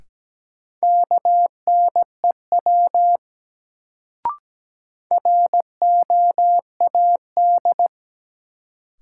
Morse code, heard as a keyed tone, about 17 wpm, 700 Hz. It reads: KNEW ROAD
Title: Sentences from Top 400 Words - New Word Drill - 17wpm